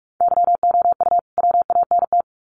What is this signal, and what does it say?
Morse code, heard as a keyed tone, about 44 wpm, 700 Hz. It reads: YOU WANT
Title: Sets of 2 Words 45wpm